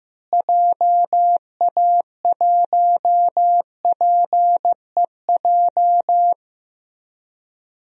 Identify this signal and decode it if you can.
Morse code, heard as a keyed tone, about 15 wpm, 700 Hz. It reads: JA1PEJ
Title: DXCC-100 Call Signs - 6 Characters - Repeated 3x - 15wpm